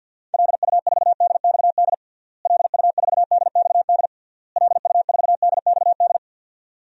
Morse code, heard as a keyed tone, about 52 wpm, 700 Hz. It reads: LU4DXD LU4DXD LU4DXD
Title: DXCC-100 Call Signs - 6 Characters - Repeated 3x - 50wpm